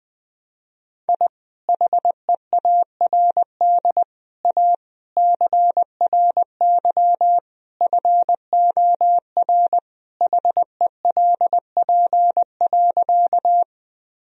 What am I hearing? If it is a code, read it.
Morse code, heard as a keyed tone, about 21 wpm, 700 Hz. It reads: I HEARD A CRY FOR HELP.
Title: Sentences from Top 1000 Words - Review Part 2 - 20wpm